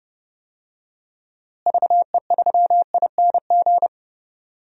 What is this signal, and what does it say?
Morse code, heard as a keyed tone, about 30 wpm, 700 Hz. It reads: VE3ING
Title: DXCC-100 Call Signs - 6 Characters - Repeated 3x - 30wpm